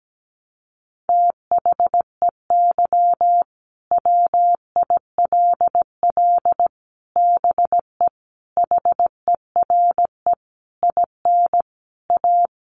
Morse code, heard as a keyed tone, about 17 wpm, 700 Hz. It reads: THEY WILL BE HERE IN A
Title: Sentences from Top 1000 Words 17wpm